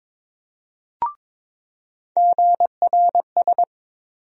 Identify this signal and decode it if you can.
Morse code, heard as a keyed tone, about 22 wpm, 700 Hz. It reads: GRS
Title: USA Call Sign Postfixes - 3 Letters 22wpm